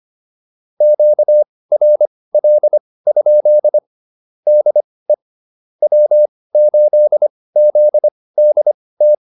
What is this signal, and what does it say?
Morse code, heard as a keyed tone, about 25 wpm, spaced as for 18 wpm, 600 Hz. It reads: QRL? DE W8ZDT